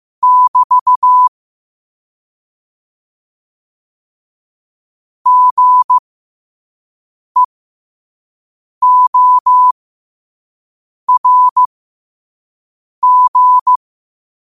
Morse code, heard as a keyed tone, about 15 wpm, spaced as for 5 wpm, 1000 Hz. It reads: <BT> GEORG
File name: qotdfiles / 5wpm